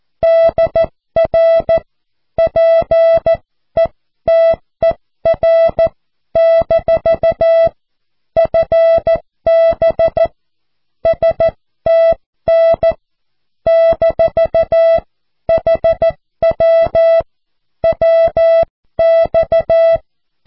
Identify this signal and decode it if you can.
Morse code, heard as a keyed tone, about 14 wpm, 660 Hz. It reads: DR PETER - FB STN - HW WX
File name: cw / QSO